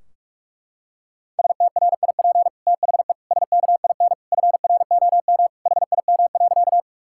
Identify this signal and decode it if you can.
Morse code, heard as a keyed tone, about 41 wpm, 700 Hz. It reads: STRIP THE SKIN FROM HIM.
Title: Sentences from Top 1600 Words 45wpm